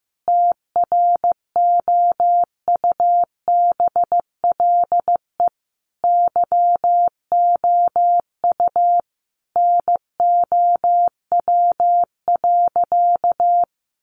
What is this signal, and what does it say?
Morse code, heard as a keyed tone, about 14 wpm, 700 Hz. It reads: TROUBLE YOU NOW.
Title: Sentences from Top 1200 Words 15wpm